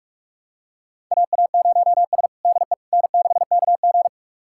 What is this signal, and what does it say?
Morse code, heard as a keyed tone, about 45 wpm, 690 Hz. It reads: AA0S DE N6KG
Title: USA Callsign DE Callsign - 4 Characters - 45wpm